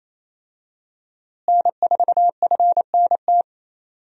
Morse code, heard as a keyed tone, about 28 wpm, 700 Hz. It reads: N4FNT
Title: USA Call Signs - 5 Characters - Repeated 2x - 28wpm